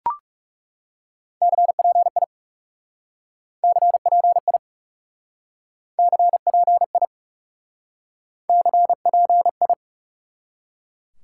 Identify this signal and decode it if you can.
Morse code, heard as a keyed tone, about 39 wpm, 700 Hz. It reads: CPI CPI CPI CPI